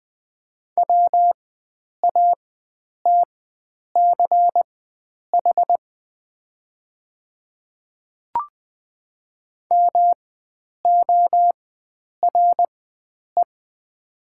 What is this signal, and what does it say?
Morse code, heard as a keyed tone, about 20 wpm, spaced as for 9 wpm, 700 Hz. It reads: WATCH MORE
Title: 4x Character Spacing - Top 300 Words - 20wpm